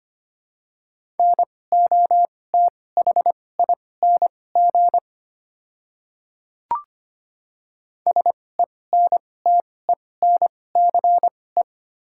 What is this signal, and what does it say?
Morse code, heard as a keyed tone, about 25 wpm, spaced as for 18 wpm, 700 Hz. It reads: NOTHING SENTENCE
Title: 2x Character Spacing - Top 500 Words - 25wpm